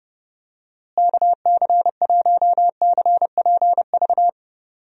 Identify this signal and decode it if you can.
Morse code, heard as a keyed tone, about 32 wpm, 700 Hz. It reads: KC1CPV
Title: USA Call Signs - 6 Characters - Repeated 3x - 30wpm